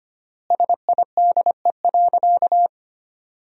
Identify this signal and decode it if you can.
Morse code, heard as a keyed tone, about 25 wpm, 700 Hz. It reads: SIDE.
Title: 3x Word-Spacing - Sentences from Top 400 Words - 25wpm